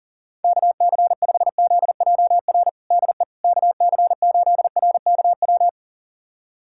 Morse code, heard as a keyed tone, about 40 wpm, 700 Hz. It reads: KC5ZJR DE KC8RKW
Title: USA Callsign DE Callsign - 6 Characters - 40wpm